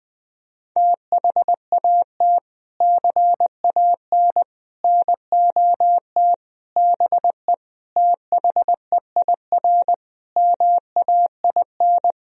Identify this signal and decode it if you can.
Morse code, heard as a keyed tone, about 20 wpm, 700 Hz. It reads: THAT CAN NOT BE THEIR MAIN
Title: Sentences from Top 1000 Words - Review Part 2 - 20wpm